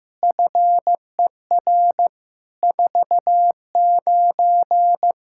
Morse code, heard as a keyed tone, about 15 wpm, 700 Hz. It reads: FER 49
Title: K1USN SST QSO Practice 15wpm